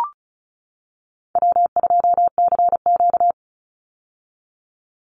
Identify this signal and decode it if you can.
Morse code, heard as a keyed tone, about 34 wpm, 700 Hz. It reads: W2CQ
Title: Call Signs - 35wpm